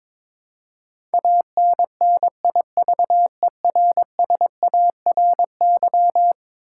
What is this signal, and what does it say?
Morse code, heard as a keyed tone, about 22 wpm, 700 Hz. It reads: ANNIVERSARY
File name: practice / 22wpm / Eleven Letter Words - 22wpm